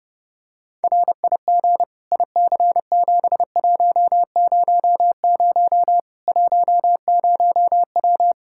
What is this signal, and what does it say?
Morse code, heard as a keyed tone, about 30 wpm, 700 Hz. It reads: RIG IC7100 10W
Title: Practice QSO - Exchange 2 - Info - 30wpm